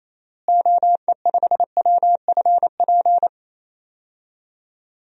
Morse code, heard as a keyed tone, about 28 wpm, 700 Hz. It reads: OE5WFP
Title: DXCC-100 Call Signs - 6 Characters - Repeated 2x - 28wpm